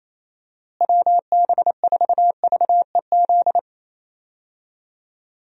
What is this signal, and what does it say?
Morse code, heard as a keyed tone, about 28 wpm, 700 Hz. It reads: WB4VEZ
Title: USA Call Signs - 6 Characters - Repeated 2x - 28wpm